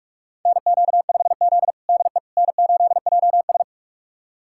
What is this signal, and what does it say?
Morse code, heard as a keyed tone, about 45 wpm, 700 Hz. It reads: NQ5Z DE N8JS
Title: USA Callsign DE Callsign - 4 Characters - 45wpm